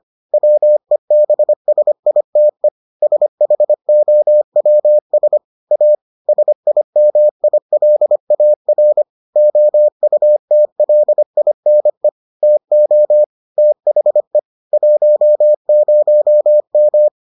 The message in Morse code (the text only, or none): WEBSITE SHOWS A SIMILAR OUTLINE TO THE 10M